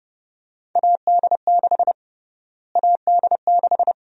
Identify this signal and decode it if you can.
Morse code, heard as a keyed tone, about 30 wpm, 700 Hz. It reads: AD6 AD6
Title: USA Call Sign Prefixes - Repeated 2x - 30wpm